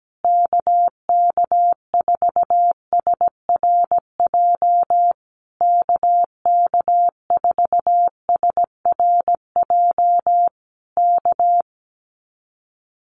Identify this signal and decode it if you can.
Morse code, heard as a keyed tone, about 17 wpm, 700 Hz. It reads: KK4SRJ KK4SRJ K
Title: Common QSO Phrases Encore 17wpm